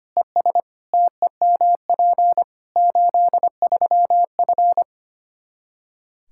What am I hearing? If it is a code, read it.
Morse code, heard as a keyed tone, about 25 wpm, 700 Hz. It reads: ES TEMP 83F
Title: Practice QSO - Exchange 2 - Info - 25wpm